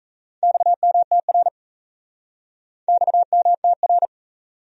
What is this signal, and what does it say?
Morse code, heard as a keyed tone, about 37 wpm, 700 Hz. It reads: XMTR XMTR